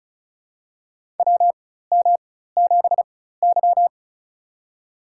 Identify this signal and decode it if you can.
Morse code, heard as a keyed tone, about 36 wpm, spaced as for 16 wpm, 700 Hz. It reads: WM7Y
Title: USA Call Signs - 4 Characters - 4x Character Spacing - 35wpm